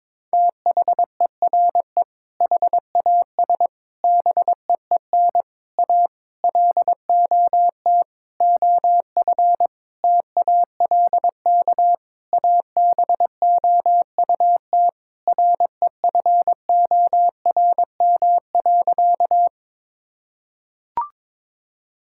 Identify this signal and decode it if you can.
Morse code, heard as a keyed tone, about 22 wpm, 700 Hz. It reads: THERE HAS BEEN A LOT OF TALK ABOUT REFORM.
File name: Sentences with 7-Word Phrases - 22wpm